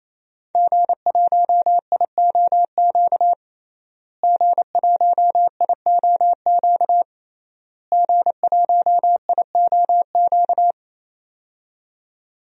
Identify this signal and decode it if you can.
Morse code, heard as a keyed tone, about 28 wpm, 700 Hz. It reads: G1IOQ G1IOQ G1IOQ
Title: DXCC-100 Call Signs - 5 Characters - Repeated 3x - 28wpm